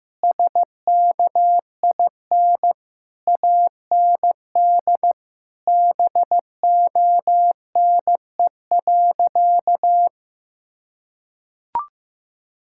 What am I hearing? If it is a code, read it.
Morse code, heard as a keyed tone, about 15 wpm, 700 Hz. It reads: SKIN AND BONE.
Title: Sentences from Top 1600 Words 15wpm